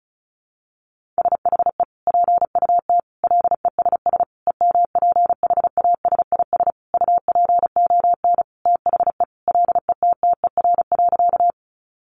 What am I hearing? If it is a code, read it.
Morse code, heard as a keyed tone, about 34 wpm, 700 Hz. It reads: SHE PUT LESS EMPHASIS UPON THE LETTER.
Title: Sentences from Top 1400 Words 35wpm